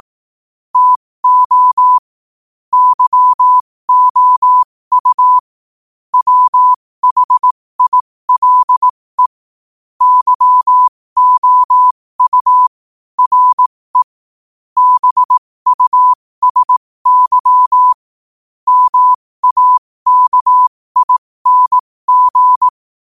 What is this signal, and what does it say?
Morse code, heard as a keyed tone, about 18 wpm, spaced as for 15 wpm, 1000 Hz. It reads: TO YOU WHILE YOU RE BUSY MAKING
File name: qotdfiles / 15wpm